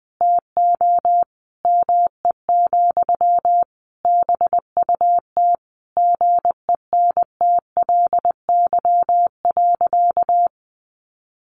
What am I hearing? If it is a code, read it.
Morse code, heard as a keyed tone, about 20 wpm, 700 Hz. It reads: TO ME, BUT GENTLY.
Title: Sentences from Top 1500 Words 20wpm